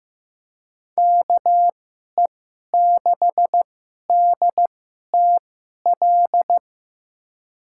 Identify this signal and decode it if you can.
Morse code, heard as a keyed tone, about 15 wpm, spaced as for 11 wpm, 700 Hz. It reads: KE6DTL
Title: USA Call Signs - 6 Characters - 2x Character Spacing - 15wpm